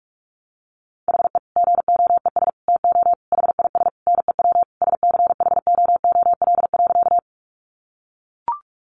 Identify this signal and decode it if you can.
Morse code, heard as a keyed tone, about 44 wpm, 700 Hz. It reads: HE GOES TO HIS NEW SCHOOL.
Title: Easy Phrases 45wpm